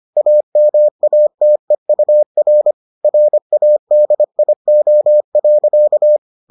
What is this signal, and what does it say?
Morse code, heard as a keyed tone, about 25 wpm, 600 Hz. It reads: AMATEUR RADIO.